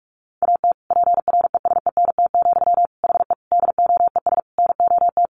Morse code, heard as a keyed tone, about 44 wpm, 700 Hz. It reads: AT PRESENT, HE DOES NOT
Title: Sentences from Top 800 Words 45wpm